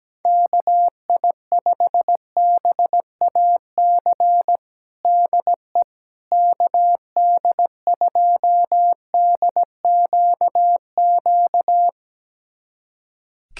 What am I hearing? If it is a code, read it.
Morse code, heard as a keyed tone, about 17 wpm, 700 Hz. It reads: KI5BAC DE KD2DQQ
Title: USA Callsign DE Callsign - 6 Characters - 17wpm